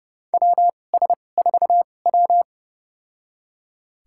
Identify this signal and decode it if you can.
Morse code, heard as a keyed tone, about 30 wpm, spaced as for 22 wpm, 690 Hz. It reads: WS4W
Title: USA Call Signs - 4 Characters - 2x Character Spacing - 30wpm